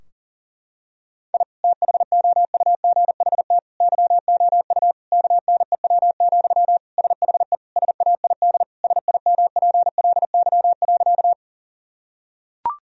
Morse code, heard as a keyed tone, about 40 wpm, 700 Hz. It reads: I THOUGHT YOU KNEW, SHE SAID SIMPLY.
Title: Sentences from Top 1000 Words - Review Part 4 - 40wpm